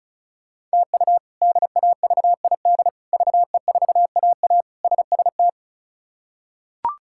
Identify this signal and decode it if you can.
Morse code, heard as a keyed tone, about 36 wpm, 700 Hz. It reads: TU DAVID VE4AA SST
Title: K1USN SST QSO Practice - 35wpm